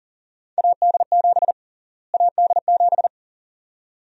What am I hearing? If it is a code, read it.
Morse code, heard as a keyed tone, about 40 wpm, 700 Hz. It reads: AD7 AD7
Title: USA Call Sign Prefixes - Repeated 2x - 40wpm